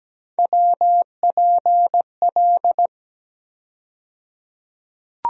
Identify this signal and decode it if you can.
Morse code, heard as a keyed tone, about 17 wpm, 700 Hz. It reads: WPL
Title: USA Call Sign Postfixes - 3 Letters 17wpm